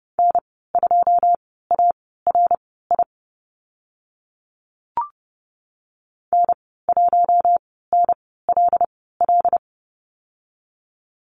Morse code, heard as a keyed tone, about 30 wpm, spaced as for 17 wpm, 700 Hz. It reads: N2ARI N1NLL
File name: Lesson 15g - Number 5 - Call Signs 30wpm